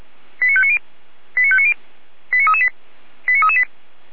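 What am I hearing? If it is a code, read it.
Five-tone selective call (ZVEI), heard as quick runs of five tones, tones of about 70 ms. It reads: 89590 89590 89338 89338
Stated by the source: ZVEI five-tone selective call received on an SDR.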